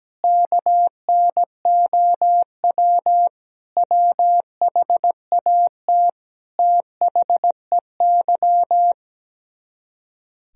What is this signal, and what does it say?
Morse code, heard as a keyed tone, about 18 wpm, 700 Hz. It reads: KNOW WHAT THEY
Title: Speed-Racing - Sets of 5 Words 17wpm